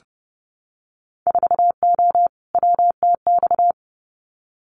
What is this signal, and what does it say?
Morse code, heard as a keyed tone, about 30 wpm, 700 Hz. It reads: 4O WTX
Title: Winter Field Day Exchanges Encore 30wpm